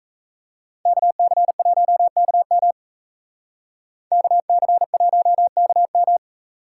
Morse code, heard as a keyed tone, about 45 wpm, 700 Hz. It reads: KC1KM KC1KM